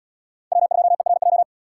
Morse code, heard as a keyed tone, about 77 wpm, 700 Hz. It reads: W2ERJ